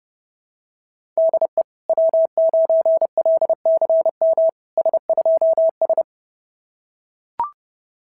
Morse code, heard as a keyed tone, about 29 wpm, 660 Hz. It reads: DE W9LCM S2S
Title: SOTA QSO Practice 30wpm